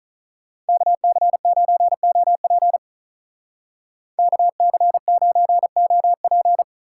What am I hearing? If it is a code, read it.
Morse code, heard as a keyed tone, about 46 wpm, 700 Hz. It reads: KC9OP KC9OP